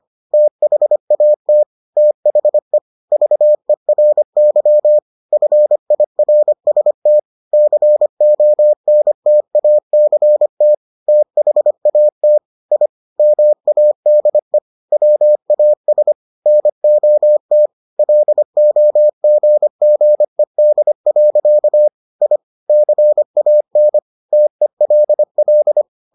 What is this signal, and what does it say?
Morse code, heard as a keyed tone, about 25 wpm, 600 Hz. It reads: THAT THE VERY FIRST CONTACT THAT I MADE WAS NOT LOGGED. I CAN TELL